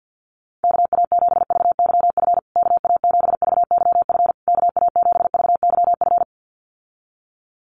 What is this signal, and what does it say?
Morse code, heard as a keyed tone, about 50 wpm, 700 Hz. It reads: KA7VYF KA7VYF KA7VYF
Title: USA Call Signs Repeated 3x 50wpm